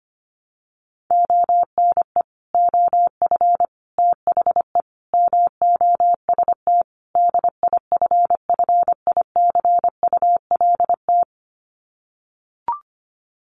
Morse code, heard as a keyed tone, about 25 wpm, 700 Hz. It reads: ONE OF THE MOST DIFFICULT
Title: Sets of 5 Words Encore2 25wpm